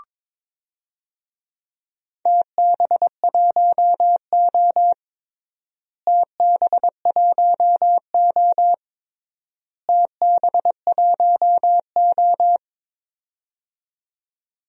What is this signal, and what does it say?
Morse code, heard as a keyed tone, about 22 wpm, 700 Hz. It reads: TB1O TB1O TB1O
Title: DXCC-100 Call Signs - 4 Characters - Repeated 3x - 22wpm